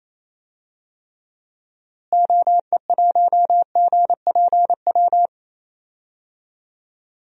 Morse code, heard as a keyed tone, about 28 wpm, 700 Hz. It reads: OE1GPW
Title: DXCC-100 Call Signs - 6 Characters - Repeated 3x - 28wpm